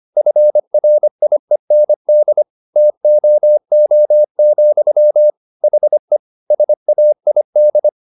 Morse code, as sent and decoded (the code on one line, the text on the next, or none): ..-. .-. .. -. -.. / - --- --- --..-- / .... / ... .- .. -..
FRIEND TOO, HE SAID